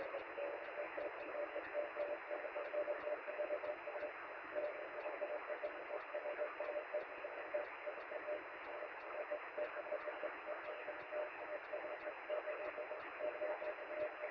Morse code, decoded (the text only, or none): none